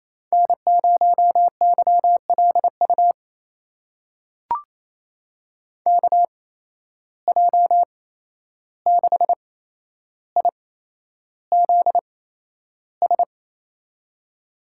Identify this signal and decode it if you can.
Morse code, heard as a keyed tone, about 28 wpm, spaced as for 11 wpm, 700 Hz. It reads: N0YLU KJ6IZS
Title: USA Call Signs - 8x Character Spacing - 28wpm